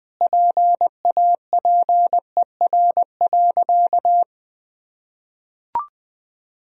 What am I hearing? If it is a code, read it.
Morse code, heard as a keyed tone, about 20 wpm, 700 Hz. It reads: PAPER.
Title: Sentences from Top 1000 Words - Review Part 2 - 20wpm